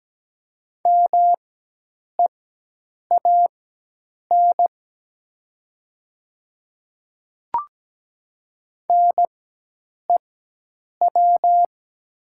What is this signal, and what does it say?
Morse code, heard as a keyed tone, about 17 wpm, spaced as for 8 wpm, 700 Hz. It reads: MEAN NEW
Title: 4x Character Spacing - Top 200 Words - 17wpm